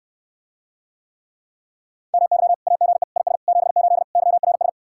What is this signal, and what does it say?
Morse code, heard as a keyed tone, about 69 wpm, 700 Hz. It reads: MY AGE IS 78 YRS